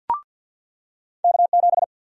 Morse code, heard as a keyed tone, about 48 wpm, 700 Hz. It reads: K7
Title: USA Call Sign Prefixes - 50wpm